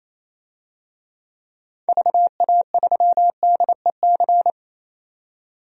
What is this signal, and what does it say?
Morse code, heard as a keyed tone, about 28 wpm, 700 Hz. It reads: VA3DEC
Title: DXCC-100 Call Signs - 6 Characters - 28wpm